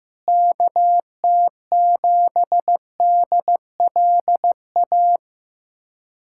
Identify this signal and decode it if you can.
Morse code, heard as a keyed tone, about 15 wpm, 700 Hz. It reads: KT7DLA
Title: USA Callsign DE Callsign 15wpm